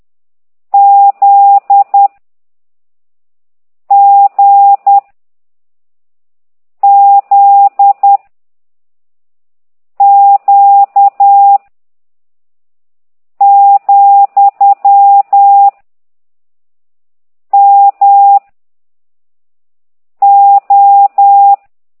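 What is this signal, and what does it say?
Morse code, heard as a keyed tone, about 10 wpm, 800 Hz. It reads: Z G Z Q , M O
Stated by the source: Morse code lesson 5 exercise 10 by CT2IDW, containing only characters M, O, G, Z, Q and comma.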